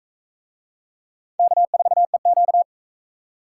Morse code, heard as a keyed tone, about 44 wpm, 700 Hz. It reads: K4EQ